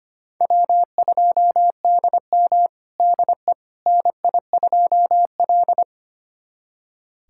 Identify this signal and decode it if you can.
Morse code, heard as a keyed tone, about 25 wpm, 700 Hz. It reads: W2DM DE NI2L
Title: USA Callsign DE Callsign - 4 Characters - 25wpm